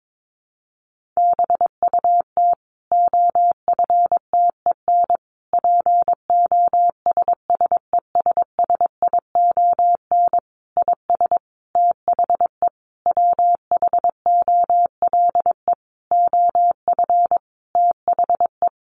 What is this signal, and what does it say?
Morse code, heard as a keyed tone, about 22 wpm, 700 Hz. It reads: BUT OFTEN POSSESSION IS THE WHOLE OF THE
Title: Sentences from Top 1400 Words 22wpm